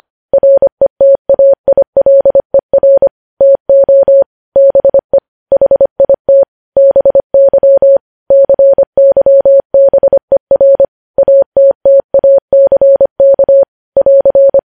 Morse code, heard as a keyed tone, about 25 wpm, 550 Hz. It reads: RETAILER TO BE HIT BY CYBER ATTACK <AR>
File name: podcast / News headlines-1x-25x25